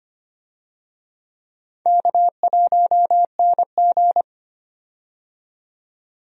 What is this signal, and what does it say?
Morse code, heard as a keyed tone, about 25 wpm, 700 Hz. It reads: K1NG
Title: USA Call Signs - 4 Characters - 25wpm